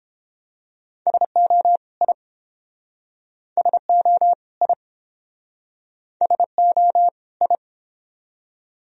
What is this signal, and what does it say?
Morse code, heard as a keyed tone, about 29 wpm, 700 Hz. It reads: SO I SO I SO I